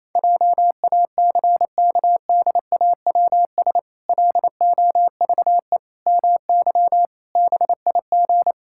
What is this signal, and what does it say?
Morse code, heard as a keyed tone, about 28 wpm, 700 Hz. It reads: JACKDAWS LOVE MY BIG
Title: Pangram Challenge v2 - 28wpm